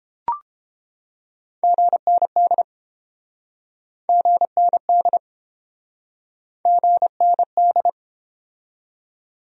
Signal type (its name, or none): Morse code